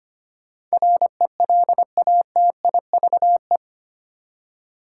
Morse code, heard as a keyed tone, about 25 wpm, 700 Hz. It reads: RELATIVE
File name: 8 Letter Words - First 3 Letters Given - 25wpm